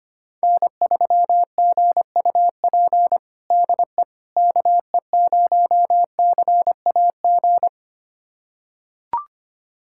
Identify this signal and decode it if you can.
Morse code, heard as a keyed tone, about 25 wpm, 700 Hz. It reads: N3GUP DE KE0CAG
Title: USA Callsign DE Callsign 25wpm